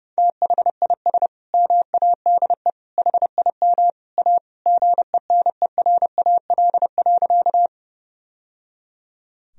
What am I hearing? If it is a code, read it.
Morse code, heard as a keyed tone, about 29 wpm, 700 Hz. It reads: THIS MADE HIM A GENERAL.
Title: Sentences from Top 1000 Words - Review Part 3 - 30wpm